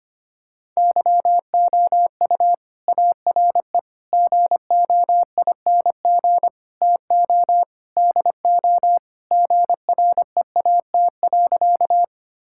Morse code, heard as a keyed tone, about 25 wpm, 700 Hz. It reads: YOU ARE GOING TO DO GREAT.
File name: practice / 25wpm / Sentences with 4-Word Phrases - 25wpm